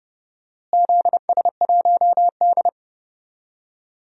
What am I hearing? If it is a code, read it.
Morse code, heard as a keyed tone, about 30 wpm, 700 Hz. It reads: ZS1D
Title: DXCC-100 Call Signs - 4 Characters - Repeated 2x - 30wpm